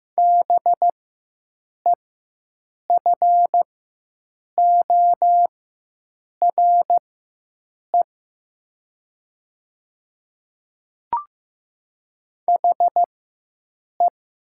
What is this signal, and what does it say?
Morse code, heard as a keyed tone, about 15 wpm, spaced as for 7 wpm, 700 Hz. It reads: BEFORE HE
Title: 4x Character Spacing - Top 200 Words - 15wpm